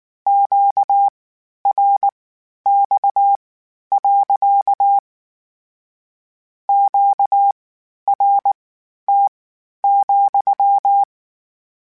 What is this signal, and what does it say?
Morse code, heard as a keyed tone, about 19 wpm, spaced as for 11 wpm, 800 Hz. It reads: QRX. QRT,
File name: Q Calls / Q Calls-mp3s / Q Calls sppc @18